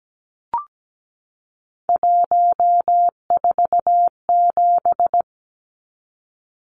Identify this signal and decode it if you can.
Morse code, heard as a keyed tone, about 17 wpm, 700 Hz. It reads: 147
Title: Numbers - 3-Digit 17wpm